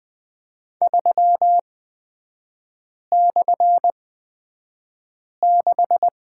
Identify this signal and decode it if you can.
Morse code, heard as a keyed tone, about 20 wpm, spaced as for 5 wpm, 700 Hz. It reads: 3/6
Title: 236 Lesson 38e - Letter q - Three Characters 20wpm